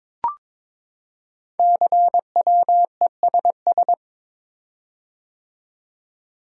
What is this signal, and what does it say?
Morse code, heard as a keyed tone, about 23 wpm, 700 Hz. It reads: CWESS
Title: Code Groups - 5 Letters - 22wpm